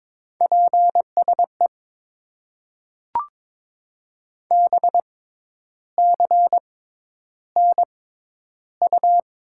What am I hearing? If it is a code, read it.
Morse code, heard as a keyed tone, about 22 wpm, spaced as for 10 wpm, 700 Hz. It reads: PSE BCNU
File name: Top 90 QSO Elements - 6x Character Spacing - 22wpm